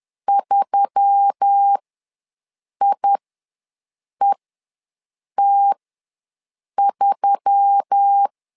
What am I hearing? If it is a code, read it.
Morse code, heard as a keyed tone, about 10 wpm, spaced as for 6 wpm, 800 Hz. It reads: 3IET3
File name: LEZIONI / 4 Q1S3 ws 10wpm 2437